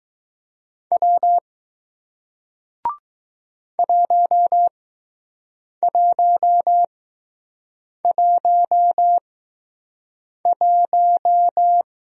Morse code, heard as a keyed tone, about 20 wpm, spaced as for 6 wpm, 700 Hz. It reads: W 1111